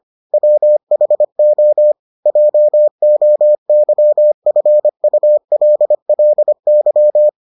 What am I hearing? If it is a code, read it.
Morse code, heard as a keyed tone, about 25 wpm, 600 Hz. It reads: WHO JOYFULLY